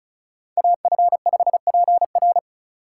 Morse code, heard as a keyed tone, about 36 wpm, 700 Hz. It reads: AF5PR
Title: USA Call Signs - 3x Character Spacing - 35wpm